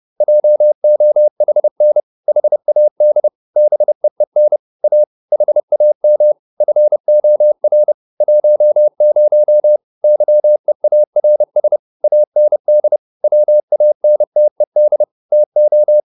Morse code, heard as a keyed tone, about 30 wpm, 600 Hz. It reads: JOHN HAD BEEN A HAM FOR 10 YEARS AND WANTED TO